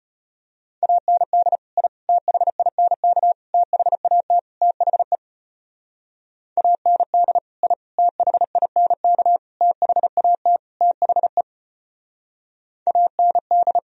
Morse code, heard as a keyed tone, about 31 wpm, 700 Hz. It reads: AND I THINK THAT THE AND I THINK THAT THE AND
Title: Speed-Racing - Sets of 5 Words 25wpm